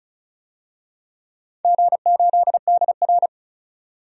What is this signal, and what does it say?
Morse code, heard as a keyed tone, about 36 wpm, 690 Hz. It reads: G8DR
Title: DXCC-100 Call Signs - 4 Characters - 35wpm